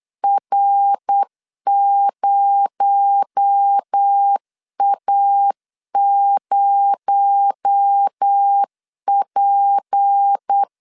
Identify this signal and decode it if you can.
Morse code, heard as a keyed tone, about 8 wpm, 800 Hz. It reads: R0A0P
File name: LEZIONI / 6 PKX0 / 6 PKX0 8wpm 3341